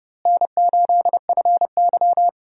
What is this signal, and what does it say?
Morse code, heard as a keyed tone, about 31 wpm, 700 Hz. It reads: N8FY
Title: USA Call Signs - 4 Characters - Repeated 3x - 30wpm